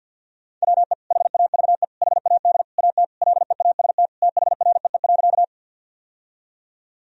Morse code, heard as a keyed tone, about 46 wpm, 700 Hz. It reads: WE HAVE HAD AT LEAST THREE.
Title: Sentences from Top 1000 Words - Review Part 1 - 50wpm